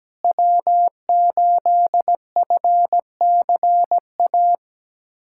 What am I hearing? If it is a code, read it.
Morse code, heard as a keyed tone, about 17 wpm, 700 Hz. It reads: W8FCA
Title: USA Callsign DE Callsign - 5 Characters - 17wpm